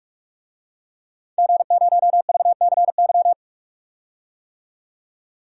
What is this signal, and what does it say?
Morse code, heard as a keyed tone, about 43 wpm, 700 Hz. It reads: G0VCY
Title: DXCC-100 Call Signs - 5 Characters - 45wpm